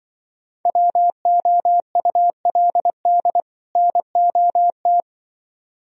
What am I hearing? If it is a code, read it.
Morse code, heard as a keyed tone, about 24 wpm, 700 Hz. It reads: WOULD NOT
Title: Speed-Racing - Sets of 2 Words 20wpm